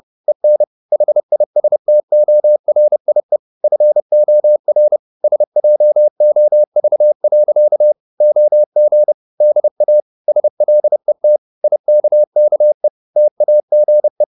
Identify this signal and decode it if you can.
Morse code, heard as a keyed tone, about 30 wpm, 600 Hz. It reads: EN HISTORIE FOR SJOV. OG DA SLET IKKE TAGE
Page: Ferie 30 wpm